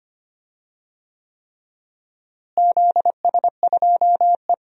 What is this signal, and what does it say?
Morse code, heard as a keyed tone, about 25 wpm, 700 Hz. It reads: ZS2E
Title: DXCC-100 Call Signs - 4 Characters - Repeated 3x - 25wpm